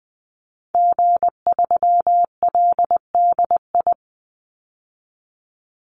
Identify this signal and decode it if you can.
Morse code, heard as a keyed tone, about 20 wpm, 700 Hz. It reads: G3LDI
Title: Call Signs Repeated 2x - 20wpm